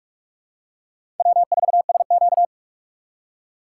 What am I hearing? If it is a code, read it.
Morse code, heard as a keyed tone, about 45 wpm, 700 Hz. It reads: W4SQ
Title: USA Call Signs - 6x Character Spacing - 45wpm